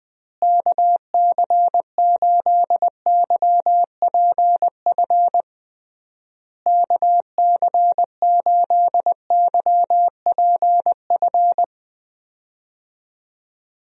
Morse code, heard as a keyed tone, about 20 wpm, 700 Hz. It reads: KC8YPF KC8YPF
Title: USA Call Signs - 6 Characters - Repeated 3x - 20wpm